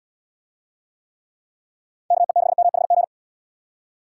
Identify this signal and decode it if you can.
Morse code, heard as a keyed tone, about 79 wpm, 700 Hz. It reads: KE6MHG